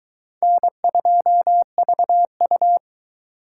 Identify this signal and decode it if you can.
Morse code, heard as a keyed tone, about 22 wpm, 700 Hz. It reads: N2VU